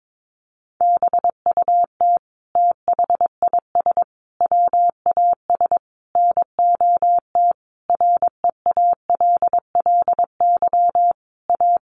Morse code, heard as a keyed tone, about 22 wpm, 700 Hz. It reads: BUT THIS WAS NOT REALLY A
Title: Sentences from Top 1500 Words 22wpm